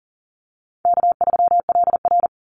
Morse code, heard as a keyed tone, about 40 wpm, 700 Hz. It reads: K3LR
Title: Easy Call Signs - 40wpm